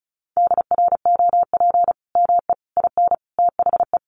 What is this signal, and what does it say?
Morse code, heard as a keyed tone, about 34 wpm, 700 Hz. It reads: DROP ME IN THE
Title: Sentences from Top 1400 Words 35wpm